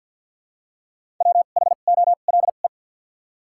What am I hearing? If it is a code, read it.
Morse code, heard as a keyed tone, about 54 wpm, spaced as for 37 wpm, 700 Hz. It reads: WHOLE
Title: 2x Character Spacing - Top 500 Words - 50wpm